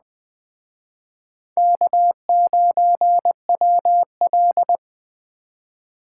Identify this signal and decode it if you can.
Morse code, heard as a keyed tone, about 20 wpm, 700 Hz. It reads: K9WL
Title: Speed Racing - USA Call Signs 17wpm